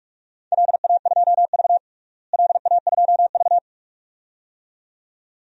Morse code, heard as a keyed tone, about 47 wpm, 700 Hz. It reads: LA2V LA2V
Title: DXCC-100 Call Signs - 4 Characters - Repeated 2x - 45wpm